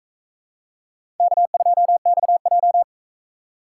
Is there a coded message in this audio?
Yes: Morse code from 1.2 to 2.8 s, heard as a keyed tone, about 42 wpm, 700 Hz. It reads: K2XJ